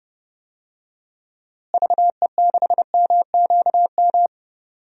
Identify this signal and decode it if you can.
Morse code, heard as a keyed tone, about 32 wpm, 700 Hz. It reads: VE6MQM
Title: DXCC-100 Call Signs - 6 Characters - Repeated 2x - 30wpm